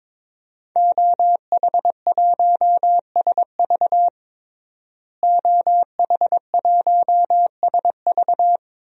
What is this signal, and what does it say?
Morse code, heard as a keyed tone, about 22 wpm, 700 Hz. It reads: OH1SV OH1SV
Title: DXCC-100 Call Signs - 5 Characters - Repeated 2x - 22wpm